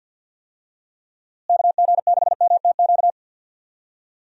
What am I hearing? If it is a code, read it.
Morse code, heard as a keyed tone, about 50 wpm, 700 Hz. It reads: KG6MTQ
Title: USA Call Signs - 6 Characters - Repeated 3x - 50wpm